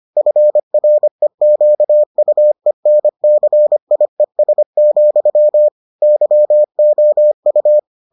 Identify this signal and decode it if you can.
Morse code, heard as a keyed tone, about 25 wpm, 600 Hz. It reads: FREQUENCIES, YOU